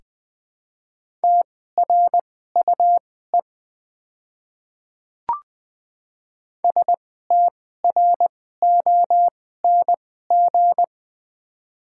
Morse code, heard as a keyed tone, about 20 wpm, spaced as for 14 wpm, 700 Hz. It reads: TRUE STRONG
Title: 2x Character Spacing - Top 500 Words - 20wpm